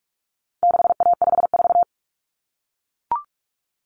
Morse code, heard as a keyed tone, about 44 wpm, 700 Hz. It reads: 6A54
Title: Common Cut Numbers - 4-Digit 45wpm